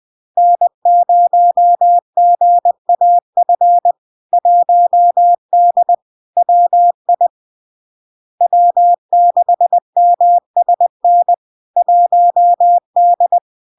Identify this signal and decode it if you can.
Morse code, heard as a keyed tone, about 20 wpm, 700 Hz. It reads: N0GAF 1D WI W6MSN 1D
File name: LARK-FD-2022-Log-in-CW-20wpm-700-Hz